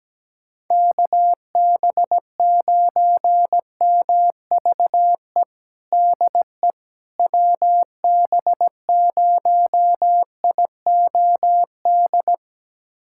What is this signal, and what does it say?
Morse code, heard as a keyed tone, about 17 wpm, 700 Hz. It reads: KB9MVE DE WB0IOD
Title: USA Callsign DE Callsign 17wpm